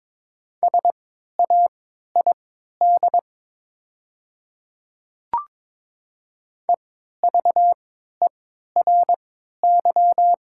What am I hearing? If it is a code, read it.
Morse code, heard as a keyed tone, about 22 wpm, spaced as for 13 wpm, 700 Hz. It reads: SAID EVERY
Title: 3x Character Spacing - Top 200 Words - 22wpm